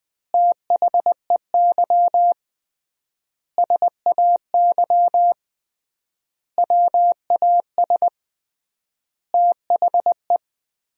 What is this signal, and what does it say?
Morse code, heard as a keyed tone, about 21 wpm, 700 Hz. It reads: THEY SAY WAS THE
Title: 3x Word-Spacing - Sentences from Top 200 Words - 20wpm